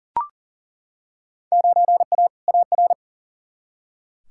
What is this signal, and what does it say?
Morse code, heard as a keyed tone, about 40 wpm, 700 Hz. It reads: 9A AR
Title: Field Day Exchanges Encore - 40wpm